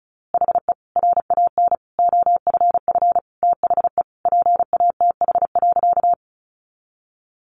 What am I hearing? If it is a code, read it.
Morse code, heard as a keyed tone, about 34 wpm, 700 Hz. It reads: HE RAN OFF THE PATH.
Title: A Sound Of Thunder 35wpm